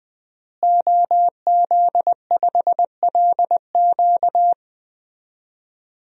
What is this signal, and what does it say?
Morse code, heard as a keyed tone, about 20 wpm, 700 Hz. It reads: OZ5LQ
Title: DXCC-100 Call Signs - 5 Characters - Repeated 3x - 20wpm